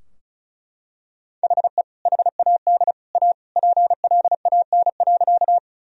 Morse code, heard as a keyed tone, about 36 wpm, 700 Hz. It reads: HE HAD A PLAN.
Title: Sentences from Top 1000 Words - Review Part 2 - 35wpm